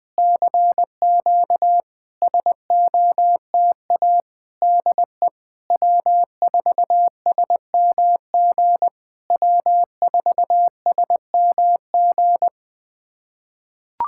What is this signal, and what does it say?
Morse code, heard as a keyed tone, about 20 wpm, 700 Hz. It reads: CQ SOTA DE W4SMG W4SMG
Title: SOTA QSO Practice 20wpm